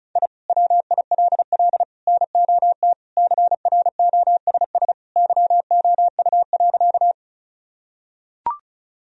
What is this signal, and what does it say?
Morse code, heard as a keyed tone, about 36 wpm, 700 Hz. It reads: I WILL NOT CROSS YOU.
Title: Sentences from Top 1000 Words - Review Part 1 - 35wpm